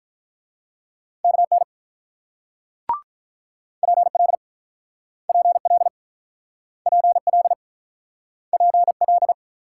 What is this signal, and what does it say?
Morse code, heard as a keyed tone, about 48 wpm, 700 Hz. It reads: KN PL PL PL PL